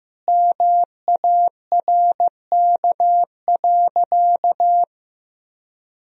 Morse code, heard as a keyed tone, about 15 wpm, 700 Hz. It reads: MARK.
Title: Sentences from Top 1000 Words - Review Part 1 - 15wpm